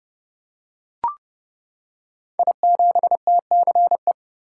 Morse code, heard as a keyed tone, about 30 wpm, 700 Hz. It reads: I7TCE
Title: Code Groups - 5 Letter:Numbers - 30wpm